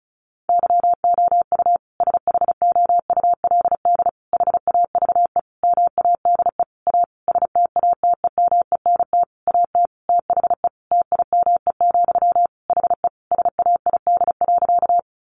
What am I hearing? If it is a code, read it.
Morse code, heard as a keyed tone, about 34 wpm, 700 Hz. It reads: YOU SHOULD HAVE MADE A STATEMENT AT THE TIME, HE SAID.
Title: Sentences from Top 1200 Words 35wpm